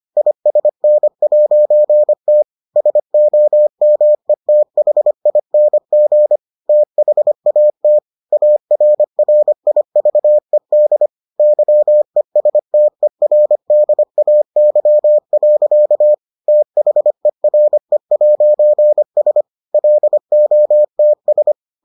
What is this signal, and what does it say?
Morse code, heard as a keyed tone, about 25 wpm, 600 Hz. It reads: ISN'T SOMETHING THAT ARRIVED YESTERDAY. THERE'S LOTS